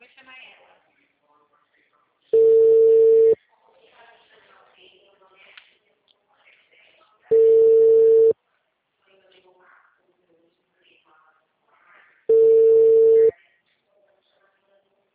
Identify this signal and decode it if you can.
Ringback tone (China, 450 Hz), with speech.
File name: out- -2016- - - .94